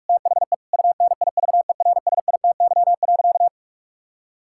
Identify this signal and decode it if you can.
Morse code, heard as a keyed tone, about 46 wpm, 700 Hz. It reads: THE UNIVERSITY.
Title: Sentences from Top 1000 Words - Review Part 5 - 45wpm